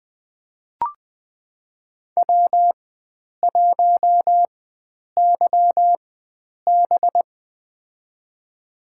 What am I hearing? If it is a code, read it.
Morse code, heard as a keyed tone, about 20 wpm, spaced as for 9 wpm, 700 Hz. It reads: W1YB
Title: USA Call Signs - 4 Characters - 4x Character Spacing - 20wpm